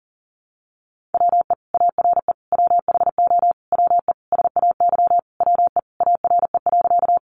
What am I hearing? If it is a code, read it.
Morse code, heard as a keyed tone, about 40 wpm, 700 Hz. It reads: WE ARE WHO WE SAY WE ARE.
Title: Sentences from Top 200 Words 40wpm